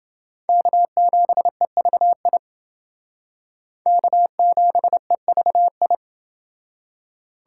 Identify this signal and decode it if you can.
Morse code, heard as a keyed tone, about 29 wpm, 700 Hz. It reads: K7EVI K7EVI